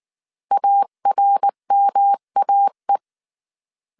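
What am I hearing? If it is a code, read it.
Morse code, heard as a keyed tone, about 19 wpm, 800 Hz. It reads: ARMAE
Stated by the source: Morse code dictation at 18 wpm from the IW3SGT QRS Morse code training course, five-character groups of 300 characters.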